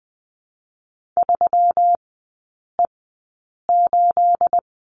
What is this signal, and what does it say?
Morse code, heard as a keyed tone, about 20 wpm, spaced as for 8 wpm, 700 Hz. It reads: 3E8
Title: 212 Lesson 34e - Number 8 - Three Characters 20wpm